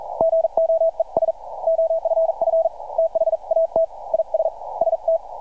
Morse code, heard as a keyed tone, about 44 wpm, 650 Hz. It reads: GOES OFF THAT IS ST
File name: archived / CRSnet